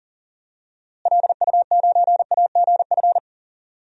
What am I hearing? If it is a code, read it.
Morse code, heard as a keyed tone, about 40 wpm, 700 Hz. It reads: LU9AGF
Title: DXCC-100 Call Signs - 6 Characters - Repeated 2x - 40wpm